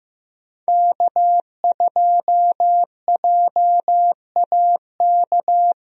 Morse code, heard as a keyed tone, about 15 wpm, 700 Hz. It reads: K2JAK
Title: USA Call Signs - 5 Characters - Repeated 2x - 15wpm